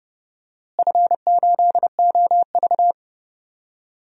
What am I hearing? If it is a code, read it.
Morse code, heard as a keyed tone, about 30 wpm, 700 Hz. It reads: F8OV
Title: DXCC-100 Call Signs - 4 Characters - Repeated 3x - 30wpm